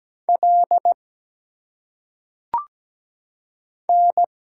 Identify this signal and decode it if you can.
Morse code, heard as a keyed tone, about 17 wpm, 700 Hz. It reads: L N
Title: Single Letters - Phonetic Pronunciation 17wpm